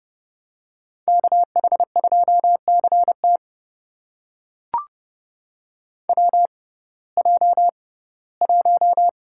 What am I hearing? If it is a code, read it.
Morse code, heard as a keyed tone, about 30 wpm, spaced as for 18 wpm, 700 Hz. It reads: KH2CT WJ1
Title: USA Call Signs - 6x Character Spacing - 30wpm